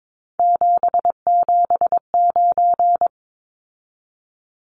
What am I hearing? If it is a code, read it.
Morse code, heard as a keyed tone, about 22 wpm, 700 Hz. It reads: 779
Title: Numbers - 3-Digit 22wpm